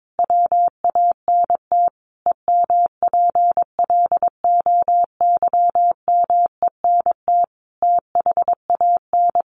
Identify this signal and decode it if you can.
Morse code, heard as a keyed tone, about 22 wpm, 700 Hz. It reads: WANT EMPLOYMENT THAN
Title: Sentences from Top 1500 Words 22wpm